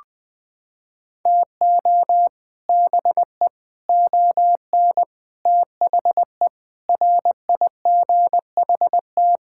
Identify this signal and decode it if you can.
Morse code, heard as a keyed tone, about 20 wpm, 700 Hz. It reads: TO BE ON THE RIGHT
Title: Sentences with 8-Word Phrases - 20wpm